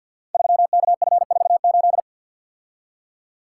Morse code, heard as a keyed tone, about 48 wpm, 700 Hz. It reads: 3KF48
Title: Code Groups - 5 Letter:Numbers - 50wpm